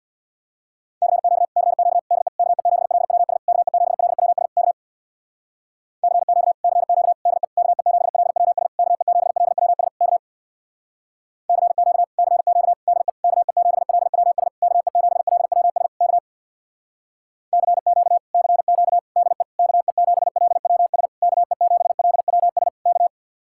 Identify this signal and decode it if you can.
Morse code, heard as a keyed tone, about 64 wpm, 700 Hz. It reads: CQ CQ DE KE7LWS KE7LWS K CQ CQ DE KE7LWS KE7LWS K CQ CQ DE KE7LWS KE7LWS K CQ CQ DE KE7LWS KE7LWS K